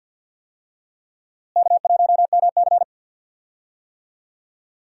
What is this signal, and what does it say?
Morse code, heard as a keyed tone, about 52 wpm, 700 Hz. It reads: K1MC
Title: USA Call Signs - 4 Characters - Repeated 3x - 50wpm